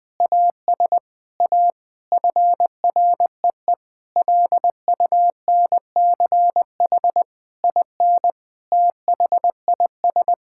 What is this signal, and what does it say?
Morse code, heard as a keyed tone, about 20 wpm, 700 Hz. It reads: AS A FREE LUNCH IN THIS
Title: Sentences with 9-Word Phrases - Spoken Beginning - 20wpm